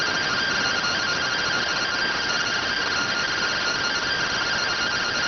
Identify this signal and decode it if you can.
Two-tone teleprinter signal, about 170 Hz shift.